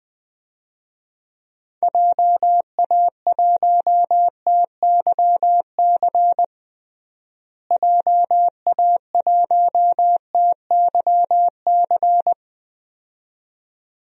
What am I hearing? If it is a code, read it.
Morse code, heard as a keyed tone, about 20 wpm, 700 Hz. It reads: JA1TYC JA1TYC
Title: DXCC-100 Call Signs - 6 Characters - Repeated 2x - 20wpm